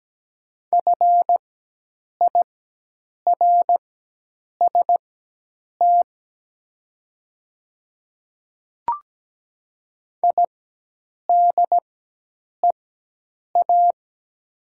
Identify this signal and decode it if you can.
Morse code, heard as a keyed tone, about 17 wpm, spaced as for 8 wpm, 700 Hz. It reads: FIRST IDEA
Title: 4x Character Spacing - Top 500 Words - 17wpm